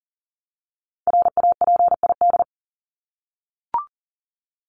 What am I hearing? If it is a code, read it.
Morse code, heard as a keyed tone, about 40 wpm, 700 Hz. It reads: RAPID
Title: LY Suffix - Top 100 Words with Base Words 40wpm